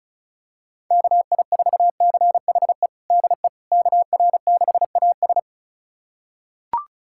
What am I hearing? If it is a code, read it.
Morse code, heard as a keyed tone, about 34 wpm, 700 Hz. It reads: KI4CHE DE KR6AS
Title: USA Callsign DE Callsign 35wpm